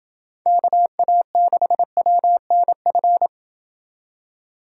Morse code, heard as a keyed tone, about 28 wpm, 700 Hz. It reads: KA6WNF